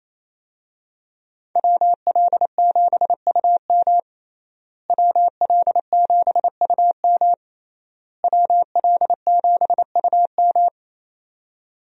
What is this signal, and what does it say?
Morse code, heard as a keyed tone, about 28 wpm, 700 Hz. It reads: WL7UM WL7UM WL7UM
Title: USA Call Signs - 5 Characters - Repeated 3x - 28wpm